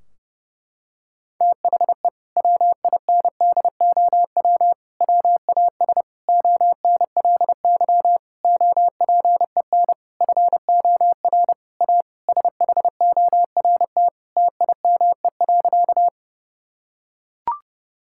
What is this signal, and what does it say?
Morse code, heard as a keyed tone, about 30 wpm, 700 Hz. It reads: THE WINDOW WAS ONLY OPEN FOR A SHORT TIME.
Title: Sentences from Top 1600 Words 30wpm